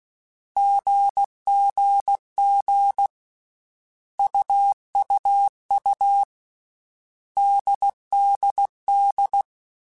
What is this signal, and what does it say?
Morse code, heard as a keyed tone, about 16 wpm, 800 Hz. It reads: GGG UUU DDD
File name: Basic Sets / Basic Letter Groups Collective7 / AWUNDG@13